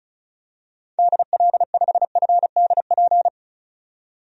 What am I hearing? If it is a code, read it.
Morse code, heard as a keyed tone, about 36 wpm, 700 Hz. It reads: DL5FDP
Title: DXCC-100 Call Signs - 6 Characters - 35wpm